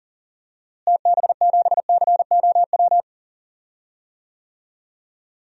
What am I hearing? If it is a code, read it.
Morse code, heard as a keyed tone, about 40 wpm, 700 Hz. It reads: TB7COW
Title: DXCC-100 Call Signs - 6 Characters - Repeated 2x - 40wpm